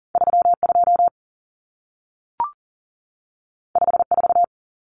Morse code, heard as a keyed tone, about 40 wpm, 700 Hz. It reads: 32 54
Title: Numbers - 2-Digit 40wpm